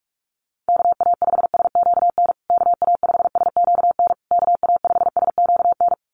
Morse code, heard as a keyed tone, about 44 wpm, 700 Hz. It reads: KA5SQN KA5SQN KA5SQN
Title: USA Call Signs Repeated 3x 45wpm